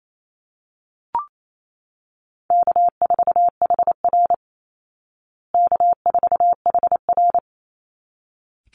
Morse code, heard as a keyed tone, about 28 wpm, 700 Hz. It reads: K4HR K4HR
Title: Call Signs Repeated 3x - 28wpm